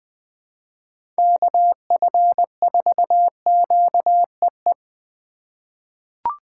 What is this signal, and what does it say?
Morse code, heard as a keyed tone, about 20 wpm, 700 Hz. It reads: KF4QEE
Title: USA Call Signs - 6 Characters - 4x Character Spacing - 20wpm